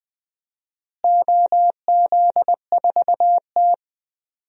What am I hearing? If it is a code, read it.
Morse code, heard as a keyed tone, about 20 wpm, 700 Hz. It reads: OZ4T
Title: DXCC-100 Call Signs - 4 Characters - Repeated 2x - 20wpm